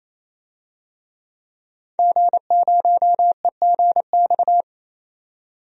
Morse code, heard as a keyed tone, about 28 wpm, 700 Hz. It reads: G0EGX
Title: DXCC-100 Call Signs - 5 Characters - 28wpm